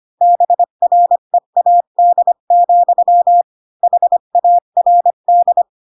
Morse code, heard as a keyed tone, about 25 wpm, 700 Hz. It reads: BREAD, HARD
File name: Pinocchio / Pinocchio ch5 20 25 ik1hge